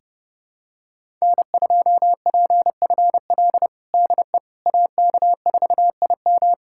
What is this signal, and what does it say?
Morse code, heard as a keyed tone, about 32 wpm, 700 Hz. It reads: N2PFL DE AK4IM
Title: USA Callsign DE Callsign - 5 Characters - 30wpm